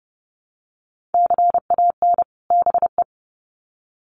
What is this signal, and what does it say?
Morse code, heard as a keyed tone, about 30 wpm, 700 Hz. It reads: CAN BE
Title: Sets of 2 Words 30wpm